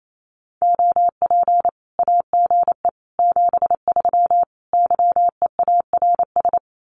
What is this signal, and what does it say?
Morse code, heard as a keyed tone, about 28 wpm, 700 Hz. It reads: OP AGE 73 YEARS
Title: Common QSO Phrases 28wpm